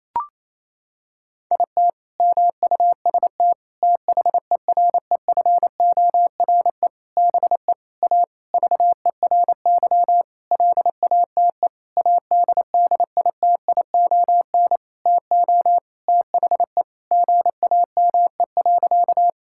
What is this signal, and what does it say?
Morse code, heard as a keyed tone, about 28 wpm, 700 Hz. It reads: IT MUST THEREFORE BE A VERY LATE ADDITION TO THE GAME.
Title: Sentences from Top 1000 Words - Review Part 6 - 28wpm